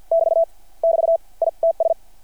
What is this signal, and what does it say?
Morse code, heard as a keyed tone, about 50 wpm, 600 Hz. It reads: <BT> <BT> I TS
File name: archived / CRSnet